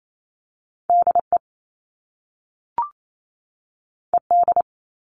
Morse code, heard as a keyed tone, about 28 wpm, 700 Hz. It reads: DE ED
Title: All Cut Numbers - 2-Digit 28wpm